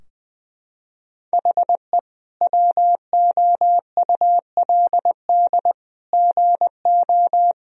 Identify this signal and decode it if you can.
Morse code, heard as a keyed tone, about 20 wpm, 700 Hz. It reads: HE WOULD GO
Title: Sentences from Top 1000 Words - Review Part 4 - 20wpm